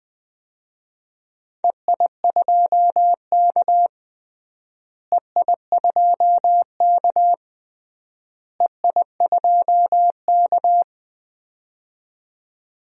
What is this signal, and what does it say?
Morse code, heard as a keyed tone, about 20 wpm, 700 Hz. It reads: EI2K EI2K EI2K
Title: DXCC-100 Call Signs - 4 Characters - Repeated 3x - 20wpm